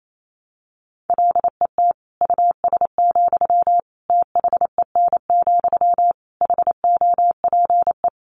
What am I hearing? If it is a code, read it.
Morse code, heard as a keyed tone, about 28 wpm, 700 Hz. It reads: LET US, THEN, HOPE
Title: Sentences from Top 1000 Words 28wpm